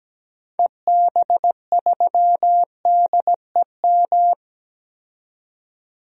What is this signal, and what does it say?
Morse code, heard as a keyed tone, about 17 wpm, 700 Hz. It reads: EB3DEM
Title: DXCC-100 Call Signs - 6 Characters - 17wpm